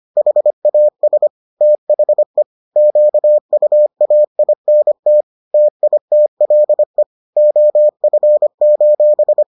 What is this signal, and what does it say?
Morse code, heard as a keyed tone, about 25 wpm, 600 Hz. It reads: HAS THE QUAINT TITLE OF: